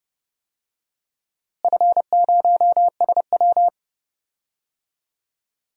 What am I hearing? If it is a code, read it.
Morse code, heard as a keyed tone, about 30 wpm, 700 Hz. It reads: F0SW
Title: DXCC-100 Call Signs - 4 Characters - Repeated 3x - 30wpm